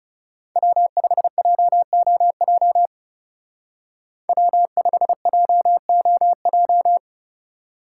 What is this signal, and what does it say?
Morse code, heard as a keyed tone, about 37 wpm, 700 Hz. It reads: W5JOJ W5JOJ
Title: Speed Racing - USA 1x3 Call Signs 30wpm